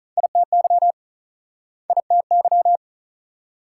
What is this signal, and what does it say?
Morse code, heard as a keyed tone, about 43 wpm, 700 Hz. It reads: ITY ITY
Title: Speed-Racing Multi-letter Phonemes - 35wpm